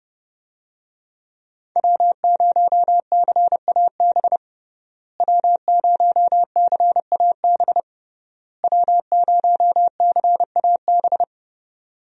Morse code, heard as a keyed tone, about 30 wpm, 700 Hz. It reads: W0CAB W0CAB W0CAB
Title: USA Call Signs - 5 Characters - Repeated 3x - 30wpm